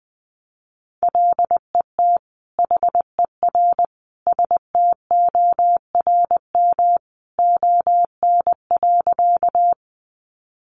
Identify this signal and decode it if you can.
Morse code, heard as a keyed tone, about 20 wpm, 700 Hz. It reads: LET HER STORM ON.
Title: Sentences from Top 1500 Words 20wpm